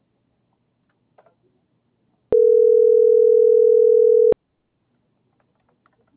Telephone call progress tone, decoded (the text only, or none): ringback tone